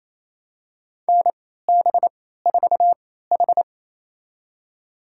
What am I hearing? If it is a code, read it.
Morse code, heard as a keyed tone, about 28 wpm, spaced as for 16 wpm, 700 Hz. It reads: NB4H
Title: USA Call Signs - 4 Characters - 3x Character Spacing - 28wpm